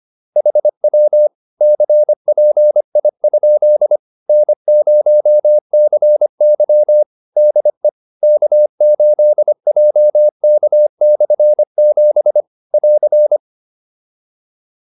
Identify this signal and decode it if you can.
Morse code, heard as a keyed tone, about 25 wpm, 600 Hz. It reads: HW CPI? N0CY DE K8JK/7 <AR>